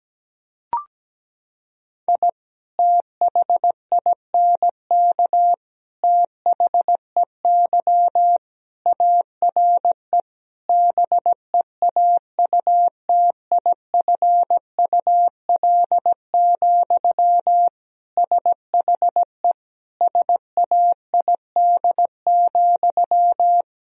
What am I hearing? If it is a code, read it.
Morse code, heard as a keyed tone, about 17 wpm, 700 Hz. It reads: I THINK THEY ARE BEAUTIFUL, SHE SAID,
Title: Sentences from Top 1000 Words - Review Part 4 - 17wpm